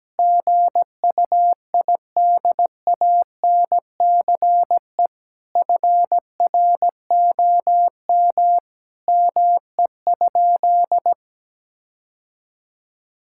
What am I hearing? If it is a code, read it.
Morse code, heard as a keyed tone, about 17 wpm, 700 Hz. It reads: GUIDANCE FROM ME?
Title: Sentences from Top 1600 Words 17wpm